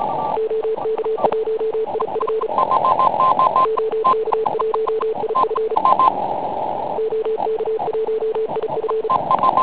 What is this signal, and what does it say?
Morse code, heard as a keyed tone, about 34 wpm, 440 Hz, with noise. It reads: OK1IF OK1IF OK1IF
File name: Pile UP CUC204